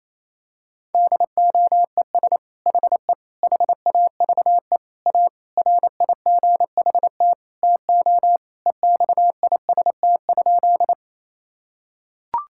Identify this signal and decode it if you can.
Morse code, heard as a keyed tone, about 28 wpm, 700 Hz. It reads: DOES HE HAVE A RIGHT TO EXIST?
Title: Sentences from Top 1000 Words - Review Part 5 - 28wpm